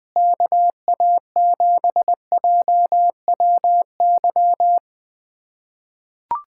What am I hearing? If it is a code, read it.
Morse code, heard as a keyed tone, about 20 wpm, 700 Hz. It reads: KA7JWY
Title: USA Callsign DE Callsign - 6 Characters - 20wpm